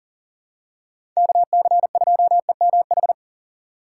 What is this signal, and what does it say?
Morse code, heard as a keyed tone, about 40 wpm, 700 Hz. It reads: KC2EMH
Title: USA Call Signs - 6 Characters - 3x Character Spacing - 40wpm